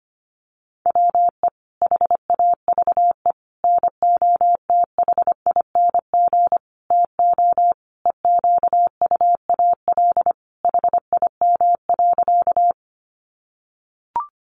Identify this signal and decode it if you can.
Morse code, heard as a keyed tone, about 25 wpm, 700 Hz. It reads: WE HAVE NOTHING TO EQUAL HIM.
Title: Sentences from Top 1400 Words 25wpm